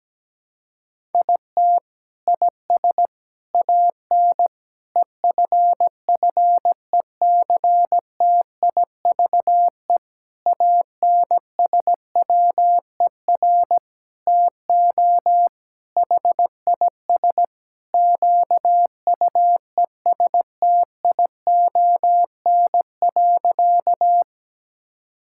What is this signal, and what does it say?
Morse code, heard as a keyed tone, about 17 wpm, 700 Hz. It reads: IT IS AN EFFECTIVE ANSWER TO HIS QUESTION.
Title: Sentences from Top 1000 Words - Review Part 6 - 17wpm